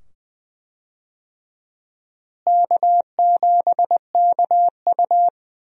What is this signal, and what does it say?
Morse code, heard as a keyed tone, about 20 wpm, 700 Hz. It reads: K7KU
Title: USA Call Signs - 4 Characters - Repeated 2x - 20wpm